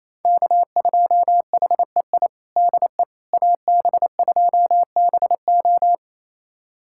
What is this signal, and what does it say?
Morse code, heard as a keyed tone, about 28 wpm, 700 Hz. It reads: K2HEI DE AB2BO
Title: USA Callsign DE Callsign - 5 Characters - 28wpm